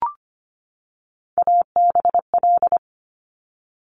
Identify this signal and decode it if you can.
Morse code, heard as a keyed tone, about 25 wpm, 700 Hz. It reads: ABL
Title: Top Three Letter Combinations in Words 25wpm